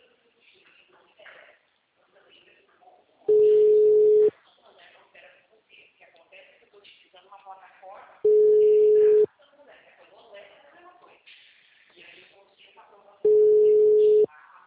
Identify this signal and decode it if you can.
Ringback tone (Europe, 425 Hz), with speech.
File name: out- -2007- - - .33538